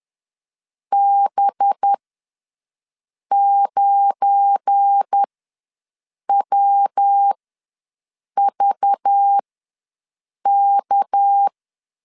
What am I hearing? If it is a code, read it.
Morse code, heard as a keyed tone, about 10 wpm, spaced as for 6 wpm, 800 Hz. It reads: B9WVK
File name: LEZIONI / 9 286 ws 10wpm 4032